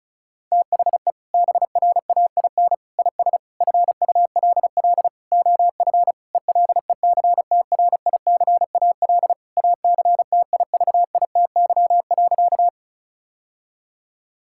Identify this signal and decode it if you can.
Morse code, heard as a keyed tone, about 34 wpm, 700 Hz. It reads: THE BRAIN IS FULL OF ELECTRICAL ACTIVITY.
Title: Sentences from Top 1000 Words - Review Part 6 - 35wpm